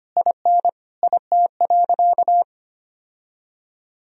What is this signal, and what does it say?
Morse code, heard as a keyed tone, about 25 wpm, 700 Hz. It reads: IN IT.
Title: Sentences from Top 1000 Words - Review Part 4 - 25wpm